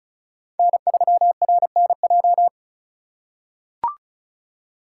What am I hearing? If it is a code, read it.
Morse code, heard as a keyed tone, about 36 wpm, 700 Hz. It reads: N3RNJ
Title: USA Call Signs - 5 Characters - 3x Character Spacing - 35wpm